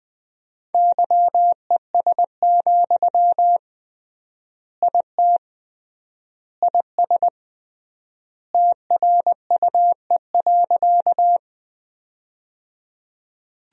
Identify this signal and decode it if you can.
Morse code, heard as a keyed tone, about 20 wpm, 700 Hz. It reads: YES, IT IS TRUE.
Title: 3x Word-Spacing - Sentences from Top 500 Words - 20wpm